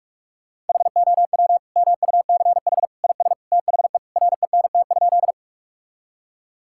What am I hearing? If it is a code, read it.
Morse code, heard as a keyed tone, about 42 wpm, 700 Hz. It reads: HOW MUCH IS THE RENT?
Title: Sentences from Top 1000 Words - Review Part 5 - 45wpm